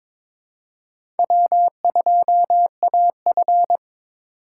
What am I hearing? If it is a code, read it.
Morse code, heard as a keyed tone, about 22 wpm, 690 Hz. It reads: W2AF
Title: USA Call Signs - 4 Characters - 3x Character Spacing - 22wpm